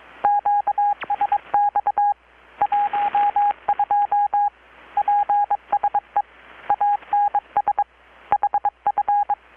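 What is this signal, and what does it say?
Morse code, heard as a keyed tone, about 22 wpm, 820 Hz, with radio noise. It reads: QSX 12 PSE ANS HF
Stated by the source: Morse from KPH's Night of Nights XX maritime radiotelegraphy event, received in Chiapas, Mexico about 3,800 km away on a Kenwood R-600 with a grounded 10 m long-wire antenna.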